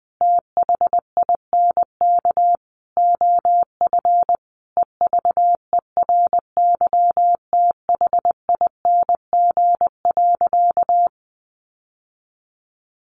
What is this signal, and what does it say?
Morse code, heard as a keyed tone, about 20 wpm, 700 Hz. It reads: THINK OF EVERYTHING.
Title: Sentences from Top 1000 Words 20wpm